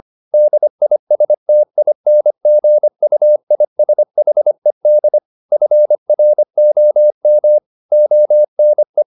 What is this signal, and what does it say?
Morse code, heard as a keyed tone, about 25 wpm, 600 Hz. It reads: DISTINGUISHED FROM ONE